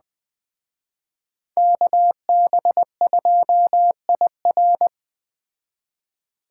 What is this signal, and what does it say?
Morse code, heard as a keyed tone, about 20 wpm, 700 Hz. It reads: KB2IR